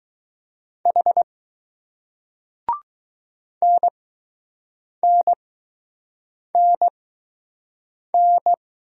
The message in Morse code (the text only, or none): H NNNN